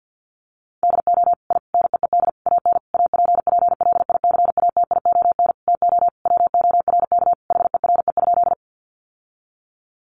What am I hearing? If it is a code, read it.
Morse code, heard as a keyed tone, about 50 wpm, 700 Hz. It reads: DO I NEED AN APPLICATION TO WORK HERE?
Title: Sentences from Top 1200 Words 50wpm